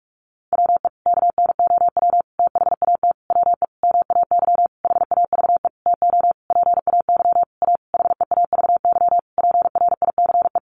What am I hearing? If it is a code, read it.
Morse code, heard as a keyed tone, about 44 wpm, 700 Hz. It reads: WE KNOW THAT WE MAY HAVE TO PAY A HEAVY PRICE
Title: Sentences from Top 1100 Words 45wpm